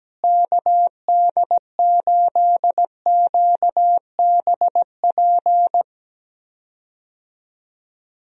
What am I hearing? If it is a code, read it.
Morse code, heard as a keyed tone, about 17 wpm, 700 Hz. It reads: KD8QBP
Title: USA Call Signs - 6 Characters - Repeated 3x - 17wpm